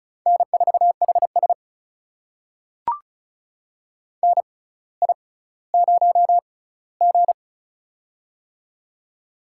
Morse code, heard as a keyed tone, about 36 wpm, spaced as for 18 wpm, 700 Hz. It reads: N4HS NI0G
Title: USA Call Signs - 4 Characters - 6x Character Spacing - 35wpm